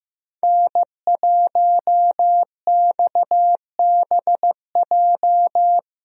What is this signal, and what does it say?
Morse code, heard as a keyed tone, about 15 wpm, 700 Hz. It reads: N1XBJ
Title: USA Call Signs - 5 Characters - Repeated 2x - 15wpm